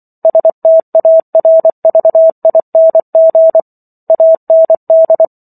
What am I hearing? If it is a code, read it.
Morse code, heard as a keyed tone, about 24 wpm, 650 Hz. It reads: STARVING AND